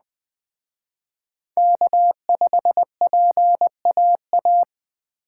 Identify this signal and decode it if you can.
Morse code, heard as a keyed tone, about 19 wpm, 700 Hz. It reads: K5PAA